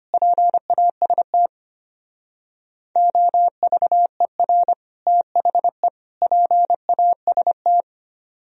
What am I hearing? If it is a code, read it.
Morse code, heard as a keyed tone, about 30 wpm, 700 Hz. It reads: PAST OVER THE PAST
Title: Speed-Racing - Sets of 3 Words 25wpm